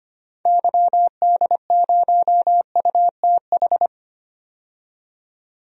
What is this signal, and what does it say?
Morse code, heard as a keyed tone, about 25 wpm, 700 Hz. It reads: YD0UTH
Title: DXCC-100 Call Signs - 6 Characters - 25wpm